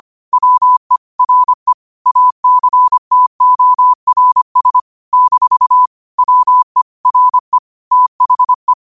Morse code, heard as a keyed tone, about 25 wpm, 1000 Hz. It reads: WE RE ACTORS - WE RE THE